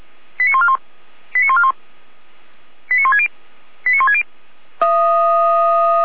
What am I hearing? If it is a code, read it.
Five-tone selective call (ZVEI), heard as quick runs of five tones, tones of about 70 ms. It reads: 89141 89141 89160 89160